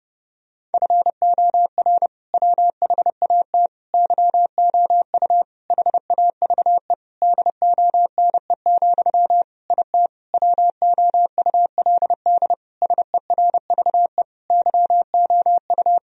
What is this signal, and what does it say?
Morse code, heard as a keyed tone, about 30 wpm, 700 Hz. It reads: FOR WHAT YOU HAVE DONE, IT WOULD SERVE YOU
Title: Sentences from Top 1000 Words - Review Part 2 - 30wpm